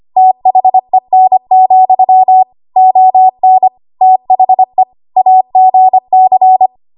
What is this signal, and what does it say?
Morse code, heard as a keyed tone, about 25 wpm, 750 Hz. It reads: THEN, ON THE AGC
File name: Archive / 25 WPM / 25WPM